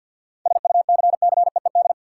Morse code, heard as a keyed tone, about 50 wpm, 700 Hz. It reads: SUCCEED
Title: Seven Letter Words - 50wpm